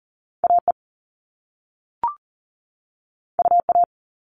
Morse code, heard as a keyed tone, about 40 wpm, 700 Hz. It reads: AE UA
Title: All Cut Numbers - 2-Digit 40wpm